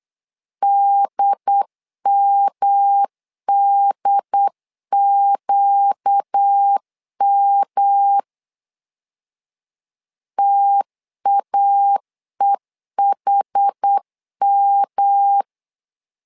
Morse code, heard as a keyed tone, about 8 wpm, 800 Hz. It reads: DMDQM TAEHM
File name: LEZIONI / 4 Q 8wpm 2732